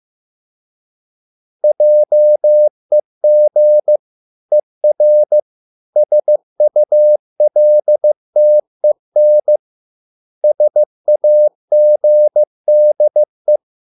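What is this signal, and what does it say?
Morse code, heard as a keyed tone, about 15 wpm, 600 Hz. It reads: JEG ER SULTEN SAGDE
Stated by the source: Morse code at a slow 15 wpm from CW med Gnister.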